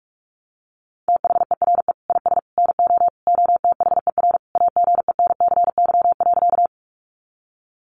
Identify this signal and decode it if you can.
Morse code, heard as a keyed tone, about 44 wpm, 700 Hz. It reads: THERE IS NO OTHER AGENCY.
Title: Sentences from Top 1300 Words 45wpm